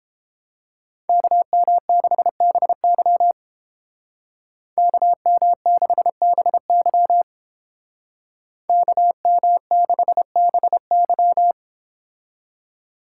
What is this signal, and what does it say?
Morse code, heard as a keyed tone, about 31 wpm, 700 Hz. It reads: KM6BY KM6BY KM6BY